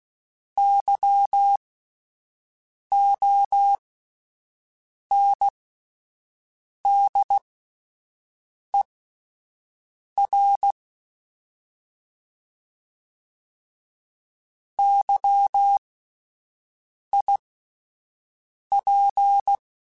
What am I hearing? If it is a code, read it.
Morse code, heard as a keyed tone, about 16 wpm, spaced as for 5 wpm, 800 Hz. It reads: YONDER YIP
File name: Alphabetical / Alphabetical mp3s / Ys @5